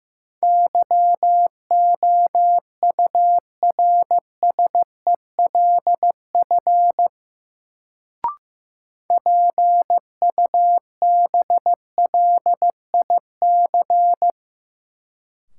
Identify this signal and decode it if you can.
Morse code, heard as a keyed tone, about 15 wpm, 700 Hz. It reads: YOURSELF PUBLIC
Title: Sentences from Top 600 Words - New Word Drill - 15wpm